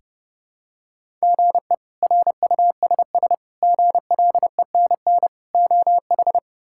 Morse code, heard as a keyed tone, about 30 wpm, 700 Hz. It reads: GE RUSS GLENN OH
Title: K1USN SST QSO Practice - 30wpm